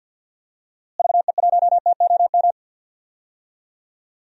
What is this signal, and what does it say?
Morse code, heard as a keyed tone, about 47 wpm, 700 Hz. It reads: VE1TOM
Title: DXCC-100 Call Signs - 6 Characters - Repeated 3x - 50wpm